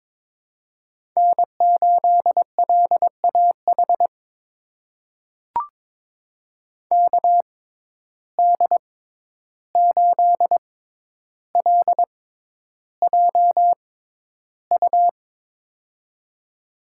Morse code, heard as a keyed tone, about 22 wpm, spaced as for 11 wpm, 700 Hz. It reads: N8LAH KD8LJU
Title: USA Call Signs - 6x Character Spacing - 22wpm